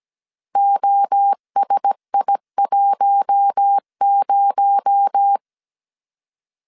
Morse code, heard as a keyed tone, about 17 wpm, 800 Hz. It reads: OSI10
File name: 16wpm